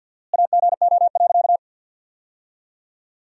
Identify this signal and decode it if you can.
Morse code, heard as a keyed tone, about 48 wpm, 700 Hz. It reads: AGO.
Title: 4x Word-Spacing - Sentences from Top 500 Words - 50wpm